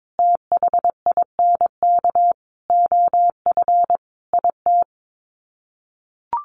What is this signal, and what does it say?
Morse code, heard as a keyed tone, about 22 wpm, 700 Hz. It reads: THINK OF IT
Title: Sets of 5 Words Encore 22wpm